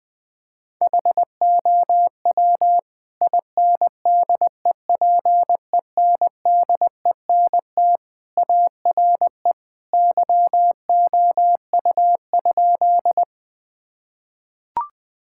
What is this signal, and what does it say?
Morse code, heard as a keyed tone, about 20 wpm, 700 Hz. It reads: HOW INDEPENDENT ARE YOU?
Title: Sentences from Top 1000 Words - Review Part 6 - 20wpm